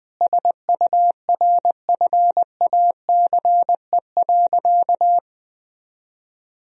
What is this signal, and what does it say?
Morse code, heard as a keyed tone, about 20 wpm, 700 Hz. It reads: SURFACE.
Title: Sentences from Top 1000 Words - Review Part 3 - 20wpm